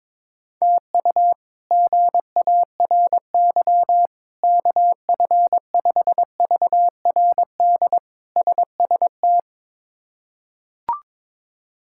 Morse code, heard as a keyed tone, about 21 wpm, 700 Hz. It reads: TU GARY KF5VRD SST
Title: K1USN SST QSO Practice - 22wpm